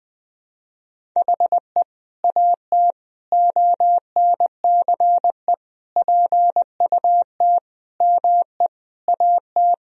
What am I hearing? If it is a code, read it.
Morse code, heard as a keyed tone, about 21 wpm, 700 Hz. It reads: HE AT ONCE PUT ME AT
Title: Sentences from Top 1000 Words - Review Part 1 - 20wpm